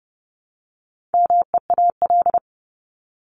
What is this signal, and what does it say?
Morse code, heard as a keyed tone, about 30 wpm, 700 Hz. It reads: MEAL
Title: Top 1751-2000 Words 30wpm